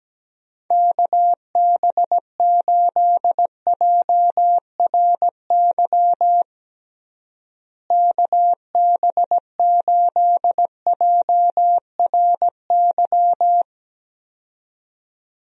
Morse code, heard as a keyed tone, about 17 wpm, 700 Hz. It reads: KB8JRY KB8JRY
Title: USA Call Signs - 6 Characters - Repeated 2x - 17wpm